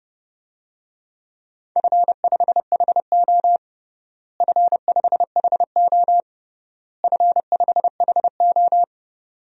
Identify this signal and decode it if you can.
Morse code, heard as a keyed tone, about 30 wpm, 700 Hz. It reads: F5HO F5HO F5HO
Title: DXCC-100 Call Signs - 4 Characters - Repeated 3x - 30wpm